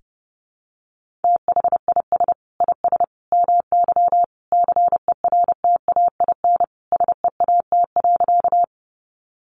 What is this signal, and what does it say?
Morse code, heard as a keyed tone, about 30 wpm, 700 Hz. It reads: THIS IS MY CERTAIN SEAT.
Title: Sentences from Top 1500 Words 30wpm